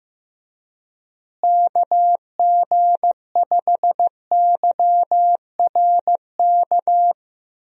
Morse code, heard as a keyed tone, about 15 wpm, 700 Hz. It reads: KG5YRK
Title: USA Call Signs - 6 Characters - Repeated 3x - 15wpm